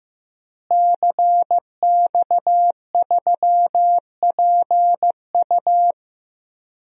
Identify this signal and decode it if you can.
Morse code, heard as a keyed tone, about 15 wpm, 700 Hz. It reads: CX3PU
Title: DXCC-100 Call Signs - 5 Characters - Repeated 2x - 15wpm